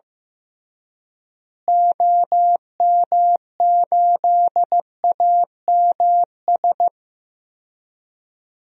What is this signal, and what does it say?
Morse code, heard as a keyed tone, about 15 wpm, 700 Hz. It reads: OM8AMS
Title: DXCC-100 Call Signs - 6 Characters - Repeated 3x - 15wpm